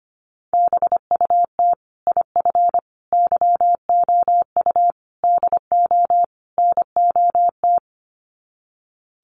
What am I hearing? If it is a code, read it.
Morse code, heard as a keyed tone, about 25 wpm, 700 Hz. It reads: BUT IF YOU DO NOT
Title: Sets of 5 Words Encore 25wpm